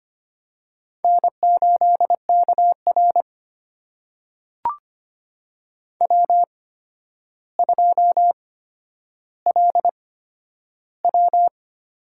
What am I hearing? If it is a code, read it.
Morse code, heard as a keyed tone, about 25 wpm, spaced as for 11 wpm, 700 Hz. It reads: N8KR W2LW
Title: USA Call Signs - 4 Characters - 8x Character Spacing - 25wpm